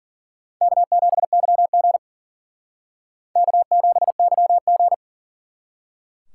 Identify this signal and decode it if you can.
Morse code, heard as a keyed tone, about 51 wpm, 700 Hz. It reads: K7YG K7YG